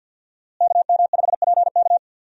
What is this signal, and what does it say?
Morse code, heard as a keyed tone, about 47 wpm, 700 Hz. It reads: KM5PK
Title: USA Call Signs - 5 Characters - Repeated 3x - 50wpm